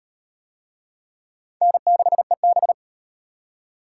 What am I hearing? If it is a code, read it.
Morse code, heard as a keyed tone, about 38 wpm, 700 Hz. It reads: N6EB